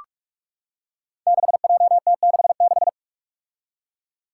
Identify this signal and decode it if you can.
Morse code, heard as a keyed tone, about 44 wpm, 700 Hz. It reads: 6JT66
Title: Five Challenging Letter-Number Combinations 45wpm